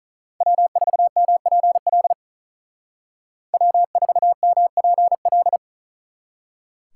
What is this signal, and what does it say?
Morse code, heard as a keyed tone, about 46 wpm, 700 Hz. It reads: W4MPL W4MPL